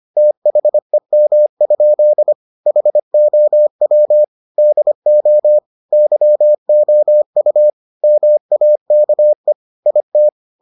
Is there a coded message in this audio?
Yes: Morse code from 0.2 to 10.3 s, heard as a keyed tone, about 25 wpm, 600 Hz. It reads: THEM? HOW DO YOU MAKE IT